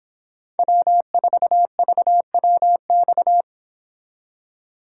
Morse code, heard as a keyed tone, about 26 wpm, 700 Hz. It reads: W4VWX